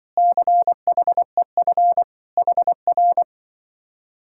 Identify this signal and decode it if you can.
Morse code, heard as a keyed tone, about 23 wpm, 700 Hz. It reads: CHEF HR
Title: Speed-Racing Common QSO Phrases - 20wpm